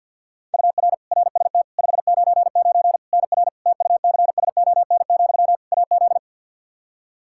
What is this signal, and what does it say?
Morse code, heard as a keyed tone, about 51 wpm, 700 Hz. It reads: UR RST 599 NR TUCSON, AZ
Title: Practice QSO - Exchange 1 - Beginning - 50wpm